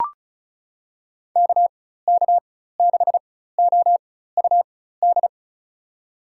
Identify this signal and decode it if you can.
Morse code, heard as a keyed tone, about 36 wpm, spaced as for 16 wpm, 700 Hz. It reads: KK6OUD
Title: USA Call Signs - 6 Characters - 4x Character Spacing - 35wpm